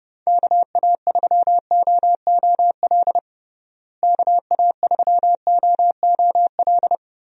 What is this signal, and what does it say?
Morse code, heard as a keyed tone, about 30 wpm, 700 Hz. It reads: KA3OOL KA3OOL
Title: USA Call Signs - 6 Characters - Repeated 3x - 30wpm